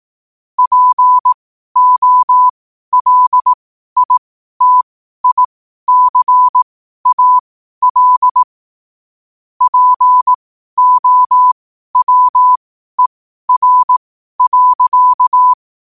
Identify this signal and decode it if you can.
Morse code, heard as a keyed tone, about 18 wpm, spaced as for 12 wpm, 1000 Hz. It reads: POLITICAL POWER.
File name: qotdfiles / 12wpm